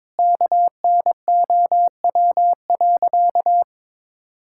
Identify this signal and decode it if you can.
Morse code, heard as a keyed tone, about 22 wpm, 700 Hz. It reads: KNOW.
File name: practice / 22wpm / Sentences with 4-Word Phrases - 22wpm